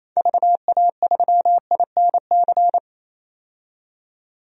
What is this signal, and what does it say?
Morse code, heard as a keyed tone, about 28 wpm, 700 Hz. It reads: VA3INC
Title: DXCC-100 Call Signs - 6 Characters - 28wpm